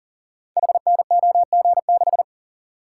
Morse code, heard as a keyed tone, about 42 wpm, 690 Hz. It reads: HNOG6
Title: Code Groups - 5 Letter:Numbers - 40wpm